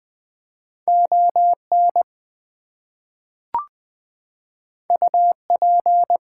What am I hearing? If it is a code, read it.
Morse code, heard as a keyed tone, about 20 wpm, 700 Hz. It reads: ON UP
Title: Two Letter Words - 20wpm